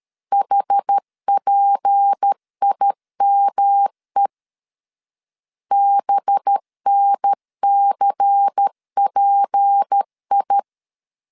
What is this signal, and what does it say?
Morse code, heard as a keyed tone, about 13 wpm, 800 Hz. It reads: HPIME BNCPI
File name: LEZIONI / 6 PKX0 / 6 P 12wpm 2033